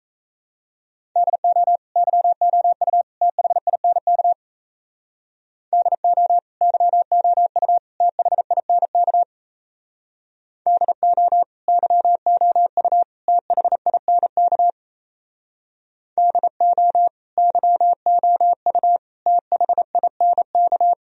Morse code, heard as a keyed tone, about 36 wpm, 700 Hz. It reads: DO YOU THINK DO YOU THINK DO YOU THINK DO YOU THINK